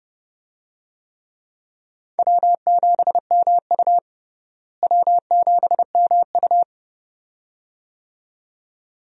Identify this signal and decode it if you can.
Morse code, heard as a keyed tone, about 30 wpm, 700 Hz. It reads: W7MU W7MU
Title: USA Call Signs - 4 Characters - Repeated 2x - 30wpm